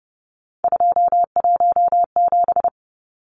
Morse code, heard as a keyed tone, about 30 wpm, 700 Hz. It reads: 217
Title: Numbers - 3-Digit 30wpm